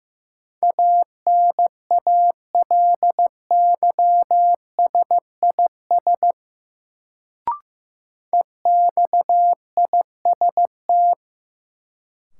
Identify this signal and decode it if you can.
Morse code, heard as a keyed tone, about 15 wpm, 700 Hz. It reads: ANALYSIS EXIST
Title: Sentences from Top 1000 Words - Review Part 5 - New Word Drill - 15wpm